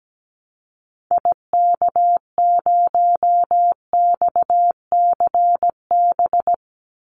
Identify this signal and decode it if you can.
Morse code, heard as a keyed tone, about 17 wpm, 700 Hz. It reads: IK0XCB
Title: Call Signs - 17wpm